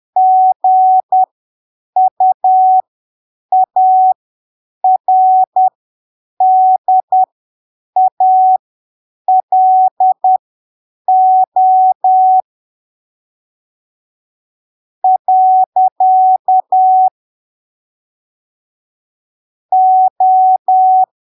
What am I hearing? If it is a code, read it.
Morse code, heard as a keyed tone, about 10 wpm, spaced as for 7 wpm, 750 Hz. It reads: GUARDALO . O
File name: arquivos / texto / texto 20 10ppm